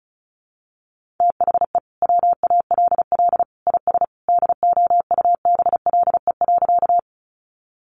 Morse code, heard as a keyed tone, about 34 wpm, 700 Hz. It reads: THE WALL IS DOUBLE.
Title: Sentences from Top 1400 Words 35wpm